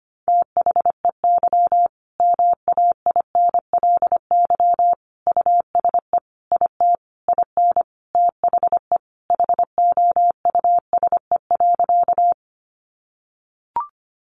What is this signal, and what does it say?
Morse code, heard as a keyed tone, about 25 wpm, 700 Hz. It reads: THEY MAINLY USE IT IN THE HOUSE.
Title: Sentences from Top 1400 Words 25wpm